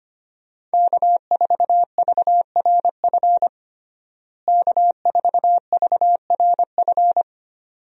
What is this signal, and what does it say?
Morse code, heard as a keyed tone, about 25 wpm, 700 Hz. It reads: K4VRF K4VRF
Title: USA Call Signs - 5 Characters - Repeated 2x - 25wpm